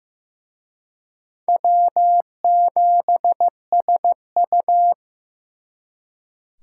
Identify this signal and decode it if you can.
Morse code, heard as a keyed tone, about 18 wpm, 700 Hz. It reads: W7SU
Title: Speed Racing - USA 1x2 Call Signs 15wpm